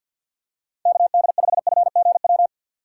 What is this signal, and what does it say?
Morse code, heard as a keyed tone, about 47 wpm, 700 Hz. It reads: KD5FGW
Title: USA Call Signs - 6 Characters - 4x Character Spacing - 50wpm